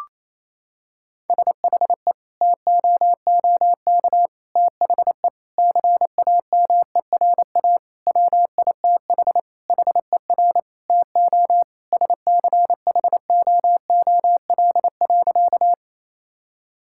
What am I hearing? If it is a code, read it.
Morse code, heard as a keyed tone, about 28 wpm, 700 Hz. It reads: SHE TOOK THE CAMERA WITH HER TO SCHOOL.
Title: Sentences from Top 1000 Words - Review Part 5 - 28wpm